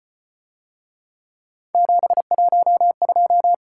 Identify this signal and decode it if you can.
Morse code, heard as a keyed tone, about 35 wpm, 700 Hz. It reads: 712